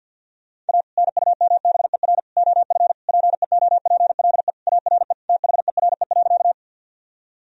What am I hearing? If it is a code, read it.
Morse code, heard as a keyed tone, about 52 wpm, 700 Hz. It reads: A NUMBER OF PEOPLE ARE THERE.
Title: Sentences from Top 1000 Words - Review Part 1 - 50wpm